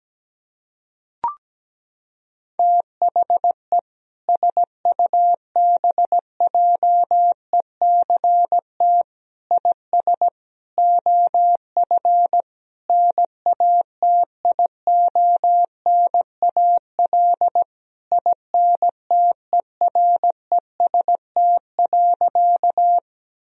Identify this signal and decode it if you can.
Morse code, heard as a keyed tone, about 17 wpm, 700 Hz. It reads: THE SUBJECT IS OF NATIONAL INTEREST.
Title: Sentences from Top 1000 Words - Review Part 4 - 17wpm